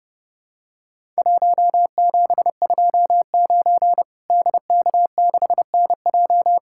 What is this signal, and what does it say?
Morse code, heard as a keyed tone, about 30 wpm, 700 Hz. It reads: 1729 DK6NJ
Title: Worked All Europe DX Contest - QTC Exchange - 30wpm